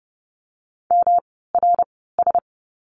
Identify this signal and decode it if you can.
Morse code, heard as a keyed tone, about 30 wpm, spaced as for 17 wpm, 700 Hz. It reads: MRS
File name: course / 30:10wpm / 108 Lesson 18f - Letter m - Words 30wpm